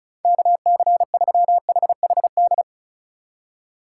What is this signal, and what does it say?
Morse code, heard as a keyed tone, about 36 wpm, 700 Hz. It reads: KC3HHD
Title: USA Call Signs - 4x Character Spacing - 35wpm